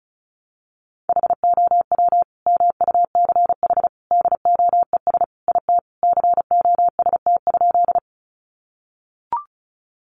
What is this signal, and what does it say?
Morse code, heard as a keyed tone, about 34 wpm, 700 Hz. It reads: HOW MUCH DOES IT COST?
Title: Sentences from Top 800 Words 35wpm